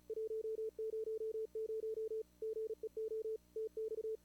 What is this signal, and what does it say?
Morse code, heard as a keyed tone, about 36 wpm, 410 Hz. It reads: 100 GEO TX
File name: CWSprint / Feb2022 / k5tr-feb-na-sprint-cw-k5ot -0000